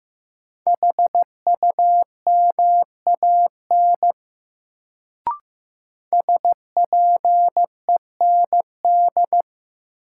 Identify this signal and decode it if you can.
Morse code, heard as a keyed tone, about 15 wpm, 700 Hz. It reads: HUMAN SPEND
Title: Sets of 5 Words - New Word Drill - 15wpm